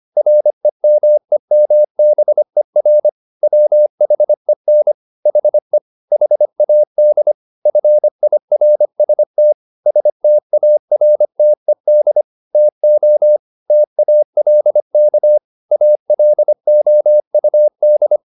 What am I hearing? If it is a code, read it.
Morse code, heard as a keyed tone, about 25 wpm, 600 Hz. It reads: REMEMBER WHEN HE HAD FIRST STARTED TO TALK ALOUD
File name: the old man and the sea part 1 25wpm